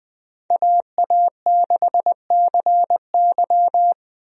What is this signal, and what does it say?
Morse code, heard as a keyed tone, about 20 wpm, 700 Hz. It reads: AA6CY
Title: USA Call Signs - 5 Characters - Repeated 3x - 20wpm